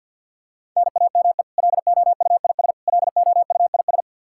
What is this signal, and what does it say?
Morse code, heard as a keyed tone, about 50 wpm, 700 Hz. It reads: NAME LOUIS LOUIS
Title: K1USN SST QSO Practice 50wpm